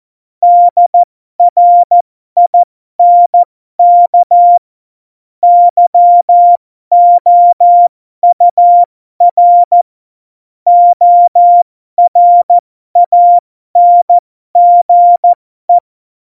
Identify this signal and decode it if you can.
Morse code, heard as a keyed tone, about 14 wpm, spaced as for 12 wpm, 700 Hz. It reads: DRINK YOUR ORANGE
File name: Morsekurs / Witze